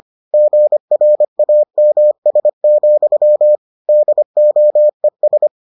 Morse code, heard as a keyed tone, about 25 wpm, 600 Hz. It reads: GRAMS, DOES